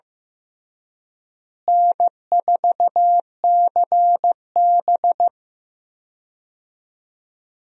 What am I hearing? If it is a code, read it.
Morse code, heard as a keyed tone, about 15 wpm, 700 Hz. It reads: N4CB
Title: USA Call Signs - 4 Characters - Repeated 3x - 15wpm